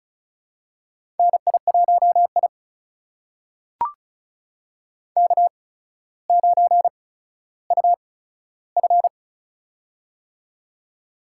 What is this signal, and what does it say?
Morse code, heard as a keyed tone, about 36 wpm, spaced as for 15 wpm, 700 Hz. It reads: NI1I K9UF
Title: USA Call Signs - 4 Characters - 8x Character Spacing - 35wpm